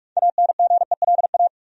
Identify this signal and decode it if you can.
Morse code, heard as a keyed tone, about 43 wpm, 700 Hz. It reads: ANGELA
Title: USA Common Names - 45wpm